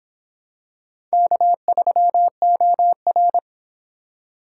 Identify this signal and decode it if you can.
Morse code, heard as a keyed tone, about 26 wpm, 700 Hz. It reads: K3OR